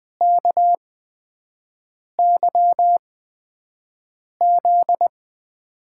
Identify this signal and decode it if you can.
Morse code, heard as a keyed tone, about 20 wpm, spaced as for 5 wpm, 700 Hz. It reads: KYZ
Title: USA Call Signs - 8x Character Spacing - 20wpm